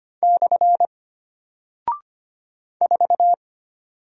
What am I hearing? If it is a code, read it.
Morse code, heard as a keyed tone, about 25 wpm, 700 Hz. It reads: / 4
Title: Single Character (Letter:Number:Punctuation:Prosign) Alt 25wpm